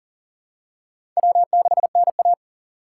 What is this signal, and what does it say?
Morse code, heard as a keyed tone, about 40 wpm, 700 Hz. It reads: W6NA
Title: USA Call Signs - 4 Characters - 4x Character Spacing - 40wpm